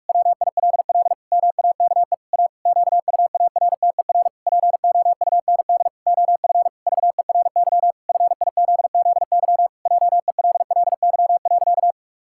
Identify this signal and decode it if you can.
Morse code, heard as a keyed tone, about 41 wpm, 700 Hz. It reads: WILL MAKE A QUARTER POUND OF VERY FIZZY JELLY.
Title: Pangram Challenge v2 - 45wpm